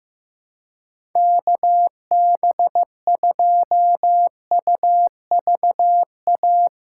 Morse code, heard as a keyed tone, about 15 wpm, 700 Hz. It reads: KB2UVA
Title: USA Call Signs - 6 Characters - 3x Character Spacing - 15wpm